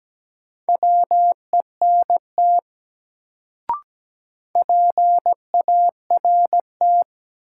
Morse code, heard as a keyed tone, about 17 wpm, 700 Hz. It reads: WENT PART
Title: Sets of 2 Words - New Word Drill - 17wpm